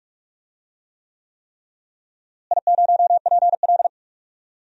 Morse code, heard as a keyed tone, about 46 wpm, 700 Hz. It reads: I0PL
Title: DXCC-100 Call Signs - 4 Characters - Repeated 3x - 45wpm